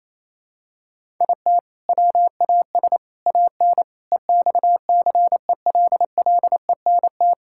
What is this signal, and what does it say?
Morse code, heard as a keyed tone, about 27 wpm, 700 Hz. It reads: IT WAS AN EXCELLENT
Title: Sentences from Top 1000 Words - Review Part 5 - 28wpm